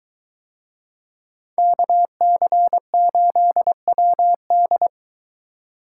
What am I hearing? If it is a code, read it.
Morse code, heard as a keyed tone, about 23 wpm, 700 Hz. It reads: KC8WD